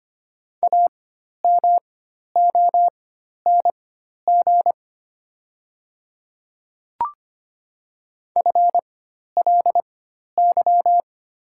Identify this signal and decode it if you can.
Morse code, heard as a keyed tone, about 25 wpm, spaced as for 12 wpm, 700 Hz. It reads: AMONG FLY
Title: 4x Character Spacing - Top 500 Words - 25wpm